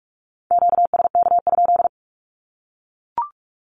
Morse code, heard as a keyed tone, about 44 wpm, 700 Hz. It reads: QSK?
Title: Q-Codes 45wpm